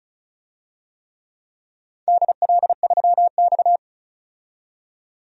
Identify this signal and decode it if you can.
Morse code, heard as a keyed tone, about 36 wpm, 700 Hz. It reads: DL3X
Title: DXCC-100 Call Signs - 4 Characters - 35wpm